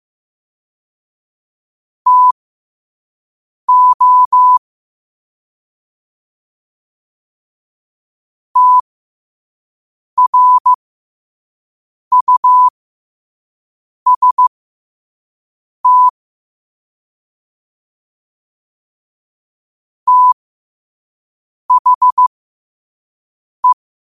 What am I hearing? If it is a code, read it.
Morse code, heard as a keyed tone, about 15 wpm, spaced as for 5 wpm, 1000 Hz. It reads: TO TRUST THE